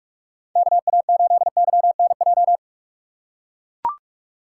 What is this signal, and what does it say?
Morse code, heard as a keyed tone, about 42 wpm, 700 Hz. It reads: KA8YNJ
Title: USA Call Signs - 6 Characters - 2x Character Spacing - 45wpm